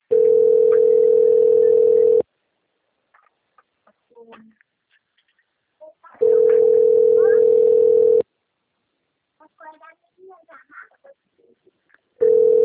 Ringback tone (North America), 440 + 480 Hz.